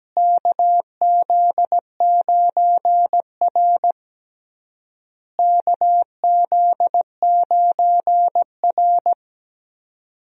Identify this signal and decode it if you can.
Morse code, heard as a keyed tone, about 17 wpm, 700 Hz. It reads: KZ9R KZ9R
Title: USA Call Signs - 4 Characters - Repeated 3x - 17wpm